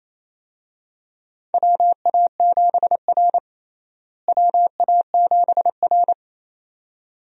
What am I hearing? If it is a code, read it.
Morse code, heard as a keyed tone, about 28 wpm, 700 Hz. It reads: WA7R WA7R
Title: USA Call Signs - 4 Characters - Repeated 2x - 28wpm